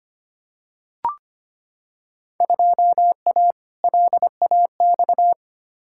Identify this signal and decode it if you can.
Morse code, heard as a keyed tone, about 25 wpm, 700 Hz. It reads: 2A LAX
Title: Field Day Exchanges - 25wpm